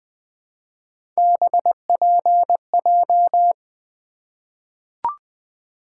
Morse code, heard as a keyed tone, about 20 wpm, 700 Hz. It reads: BPJ
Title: Three Letter-Number Combinations - 20wpm